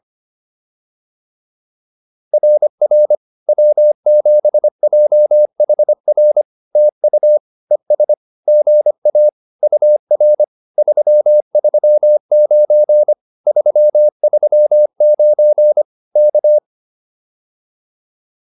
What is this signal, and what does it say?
Morse code, heard as a keyed tone, about 25 wpm, 600 Hz. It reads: RR W7JHR TU ES GA UR 339 339 K
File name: POTA307a 25